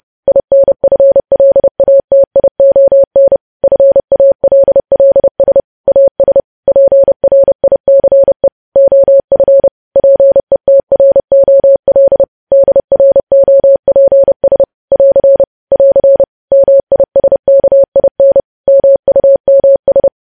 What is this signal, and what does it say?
Morse code, heard as a keyed tone, about 30 wpm, 550 Hz. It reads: INFLATION FALLS AS PRICE OF PETROL DROPS <AR> <AR> MISKIN MUMS
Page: news headlines in morse code at 30 wpm